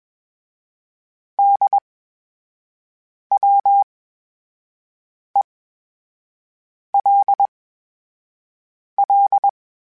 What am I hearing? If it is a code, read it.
Morse code, heard as a keyed tone, about 21 wpm, spaced as for 5 wpm, 800 Hz. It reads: DWELL
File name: Special INGs-mp3s / Special INGs-mp3s / VerySlowToFaster Ing @20